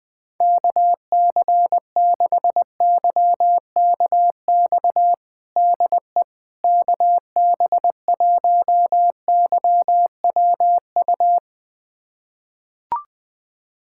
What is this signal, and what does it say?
Morse code, heard as a keyed tone, about 21 wpm, 700 Hz. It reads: KC6YKX DE KB1YWU
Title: USA Callsign DE Callsign - 6 Characters - 20wpm